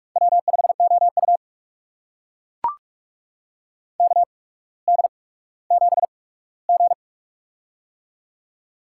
Morse code, heard as a keyed tone, about 42 wpm, spaced as for 19 wpm, 700 Hz. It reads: W5OU KD7G
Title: USA Call Signs - 4 Characters - 8x Character Spacing - 45wpm